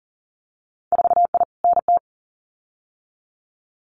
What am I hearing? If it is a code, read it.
Morse code, heard as a keyed tone, about 38 wpm, 700 Hz. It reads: 4I NT
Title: Winter Field Day Exchanges Encore 40wpm